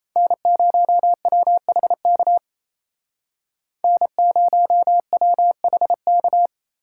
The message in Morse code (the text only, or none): N0WHK N0WHK